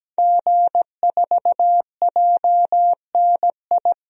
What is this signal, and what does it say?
Morse code, heard as a keyed tone, about 17 wpm, 700 Hz. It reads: G4JNI
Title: DXCC-100 Call Signs - 5 Characters - 17wpm